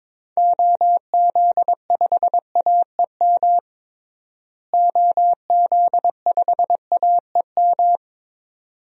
Morse code, heard as a keyed tone, about 22 wpm, 700 Hz. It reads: OZ5AEM OZ5AEM
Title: DXCC-100 Call Signs - 6 Characters - Repeated 3x - 22wpm